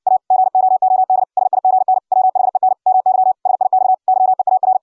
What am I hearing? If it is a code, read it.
Morse code, heard as a keyed tone, about 60 wpm, 750 Hz. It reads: A GOOD SIGN WHEN MY SIX YEAR